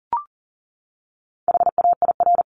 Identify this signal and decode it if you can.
Morse code, heard as a keyed tone, about 40 wpm, 700 Hz. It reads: HAIR
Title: Top 1751-2000 Words 40wpm